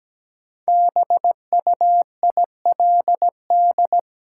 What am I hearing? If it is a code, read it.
Morse code, heard as a keyed tone, about 17 wpm, 700 Hz. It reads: BUILD
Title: Sentences from Top 1000 Words - Review Part 1 - New Word Drill - 17wpm